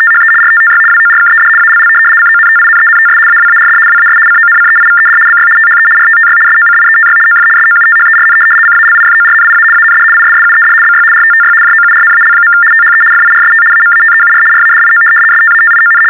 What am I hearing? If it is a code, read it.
RTTY, about 75 baud. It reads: THE RTTY WORKED ALL CONTINENTS AWARD WAS CONCEIVED BY THE RTTY SOCIETY OF SOUTHERN CALIFORNIA AND ISSUED BY RTTY JOURNAL. THE FIRST AMATEUR RADIO STATION
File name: teletype-RTTY-75